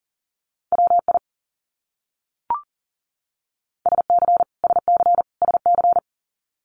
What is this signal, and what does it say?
Morse code, heard as a keyed tone, about 40 wpm, 700 Hz. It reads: WI SC SC SC
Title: US States Repeated 3x 40wpm